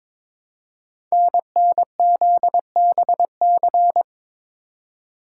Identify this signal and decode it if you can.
Morse code, heard as a keyed tone, about 22 wpm, 700 Hz. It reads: NNZBC
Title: Code Groups - 5 Letters - 22wpm